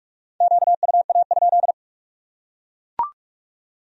Morse code, heard as a keyed tone, about 42 wpm, 700 Hz. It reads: QUA?